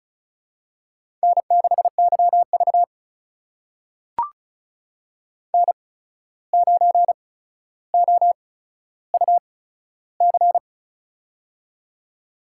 Morse code, heard as a keyed tone, about 36 wpm, spaced as for 14 wpm, 700 Hz. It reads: N6YV N9OUC
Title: USA Call Signs - 8x Character Spacing - 35wpm